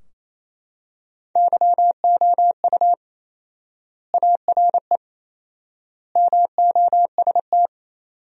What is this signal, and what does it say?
Morse code, heard as a keyed tone, about 28 wpm, 700 Hz. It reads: YOU ARE MOST
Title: 4x Word-Spacing - Sentences from Top 200 Words - 28wpm